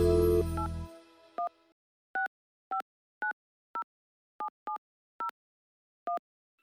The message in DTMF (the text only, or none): 8165907701